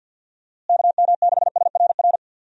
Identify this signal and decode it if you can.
Morse code, heard as a keyed tone, about 52 wpm, 700 Hz. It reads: KM6SRR
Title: USA Call Signs - 6 Characters - Repeated 3x - 50wpm